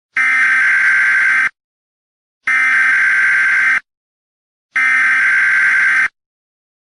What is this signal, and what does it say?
Emergency Alert System (SAME) tones (header bursts), heard as harsh buzzing bursts. It is heard from 0.1 to 6.1 s.